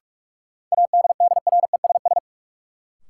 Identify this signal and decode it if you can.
Morse code, heard as a keyed tone, about 46 wpm, 700 Hz. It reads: ADDRESS
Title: Sentences from Top 1000 Words - New Word Drill - 45wpm